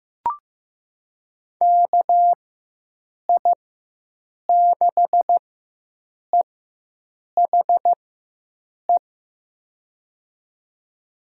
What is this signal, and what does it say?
Morse code, heard as a keyed tone, about 15 wpm, spaced as for 7 wpm, 700 Hz. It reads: KI6EHE
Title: USA Call Signs - 6 Characters - 4x Character Spacing - 15wpm